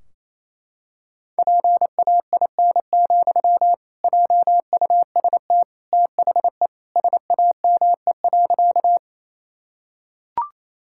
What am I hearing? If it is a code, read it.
Morse code, heard as a keyed tone, about 28 wpm, 700 Hz. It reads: PAIN, JUST THE SAME.
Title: Sentences from Top 1600 Words 28wpm